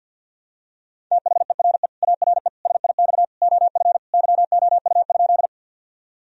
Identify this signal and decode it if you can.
Morse code, heard as a keyed tone, about 50 wpm, 700 Hz. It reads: THERE ARE SIX OF YOU?
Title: Sentences from Top 1000 Words - Review Part 2 - 50wpm